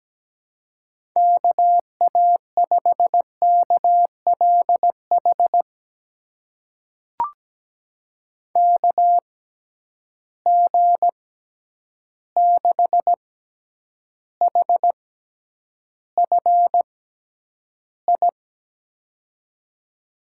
Morse code, heard as a keyed tone, about 17 wpm, spaced as for 9 wpm, 700 Hz. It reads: KA5KLH KG6HFI
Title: USA Call Signs - 6 Characters - 6x Character Spacing - 17wpm